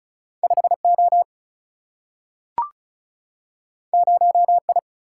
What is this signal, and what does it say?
Morse code, heard as a keyed tone, about 36 wpm, 700 Hz. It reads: 5 O 0 I
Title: Single-Letter Single-Number 35wpm